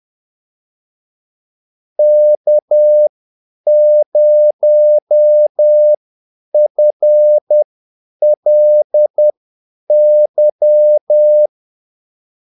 Morse code, heard as a keyed tone, about 10 wpm, spaced as for 8 wpm, 600 Hz. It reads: K0FLY
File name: Callsigns-10-8wpm